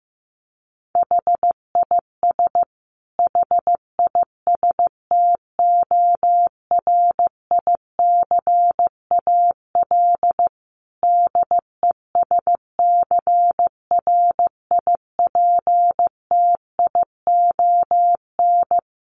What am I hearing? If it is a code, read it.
Morse code, heard as a keyed tone, about 15 wpm, 700 Hz. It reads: HIS HISTORICAL DESCRIPTION
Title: Sentences from Top 1300 Words 15wpm